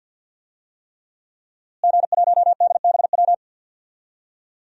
Morse code, heard as a keyed tone, about 49 wpm, 700 Hz. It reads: G1DBW
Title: DXCC-100 Call Signs - 5 Characters - Repeated 3x - 50wpm